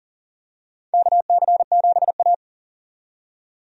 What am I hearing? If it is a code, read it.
Morse code, heard as a keyed tone, about 40 wpm, 700 Hz. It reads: KC7A
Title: USA Call Signs - 4 Characters - 40wpm